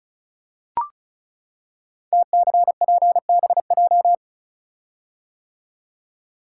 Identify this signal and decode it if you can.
Morse code, heard as a keyed tone, about 36 wpm, 690 Hz. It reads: TCPBJ
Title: Code Groups - 5 Letters - 35wpm